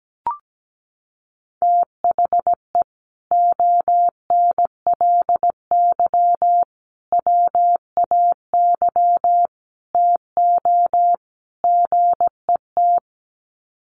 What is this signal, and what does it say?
Morse code, heard as a keyed tone, about 17 wpm, 700 Hz. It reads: THE ONLY WAY TO GET
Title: Sets of 5 Words Encore2 17wpm